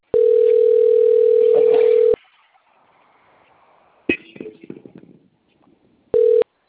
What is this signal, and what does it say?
Ringback tone (North America), 440 + 480 Hz.